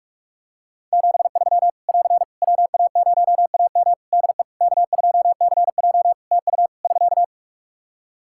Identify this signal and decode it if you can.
Morse code, heard as a keyed tone, about 46 wpm, 700 Hz. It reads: 73 <AR> WA0AM DE K2CJ TU <SK>
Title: K1USN SST QSO Practice 45wpm